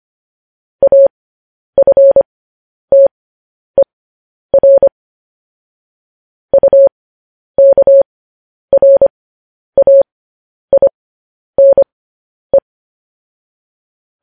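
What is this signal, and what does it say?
Morse code, heard as a keyed tone, about 25 wpm, spaced as for 10 wpm, 550 Hz. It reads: AFTER UKRAINE
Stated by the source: Morse code news headlines sent at 10.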